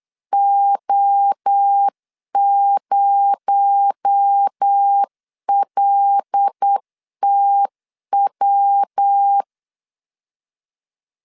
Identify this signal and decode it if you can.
Morse code, heard as a keyed tone, about 8 wpm, 800 Hz. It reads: O0LTW
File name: LEZIONI / 7 GF9Z / 7 G 8wpm 3118